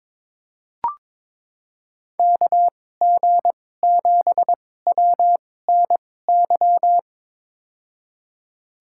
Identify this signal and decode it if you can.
Morse code, heard as a keyed tone, about 22 wpm, spaced as for 16 wpm, 700 Hz. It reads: KG7WNY
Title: USA Call Signs - 6 Characters - 2x Character Spacing - 22wpm